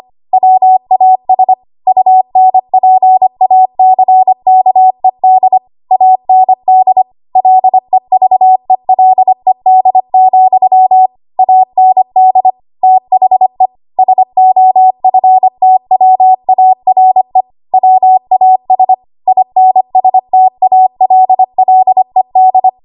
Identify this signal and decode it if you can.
Morse code, heard as a keyed tone, about 25 wpm, 750 Hz. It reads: WAS UNPACKED AND LEVELED, AND THE SOFTWARE WAS INSTALLED
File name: Archive / 25 WPM / 25WPM